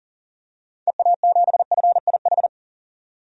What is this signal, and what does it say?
Morse code, heard as a keyed tone, about 40 wpm, 700 Hz. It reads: EA7FIH
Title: DXCC-100 Call Signs - 6 Characters - 40wpm